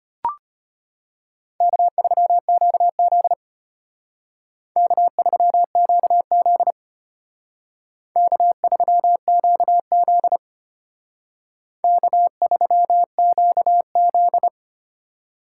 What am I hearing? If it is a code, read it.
Morse code, heard as a keyed tone, about 32 wpm, 700 Hz. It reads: K3QZ K3QZ K3QZ K3QZ